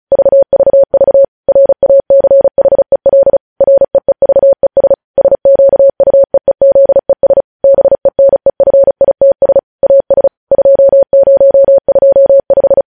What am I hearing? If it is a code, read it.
Morse code, heard as a keyed tone, about 36 wpm, 550 Hz. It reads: VVV RACHEL REEVES SQUEEZES BENEFITS AS 2025
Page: News Headlines in Morse Code at 35 WPM